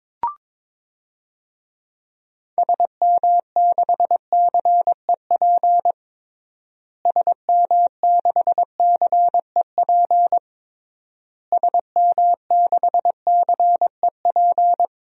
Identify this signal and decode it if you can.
Morse code, heard as a keyed tone, about 22 wpm, 700 Hz. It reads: SM6CEP SM6CEP SM6CEP
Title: DXCC-100 Call Signs - 6 Characters - Repeated 3x - 22wpm